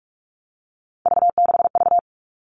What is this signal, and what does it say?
Morse code, heard as a keyed tone, about 44 wpm, 700 Hz. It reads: V6V
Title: All Cut Numbers - 3-Digit 45wpm